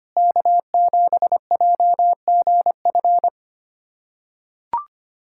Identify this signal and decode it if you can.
Morse code, heard as a keyed tone, about 25 wpm, 700 Hz. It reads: K7JGF
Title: USA Callsign DE Callsign - 5 Characters - 25wpm